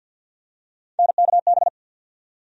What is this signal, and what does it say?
Morse code, heard as a keyed tone, about 47 wpm, 700 Hz. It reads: NKB
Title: USA Call Sign Postfixes - 3 Letters 50wpm